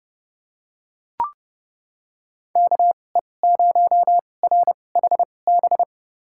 Morse code, heard as a keyed tone, about 32 wpm, spaced as for 22 wpm, 700 Hz. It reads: KE0RHB
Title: USA Call Signs - 6 Characters - 2x Character Spacing - 30wpm